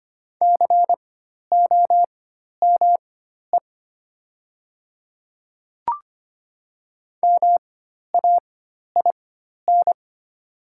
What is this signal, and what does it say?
Morse code, heard as a keyed tone, about 25 wpm, spaced as for 12 wpm, 700 Hz. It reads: COME MAIN
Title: 4x Character Spacing - Top 300 Words - 25wpm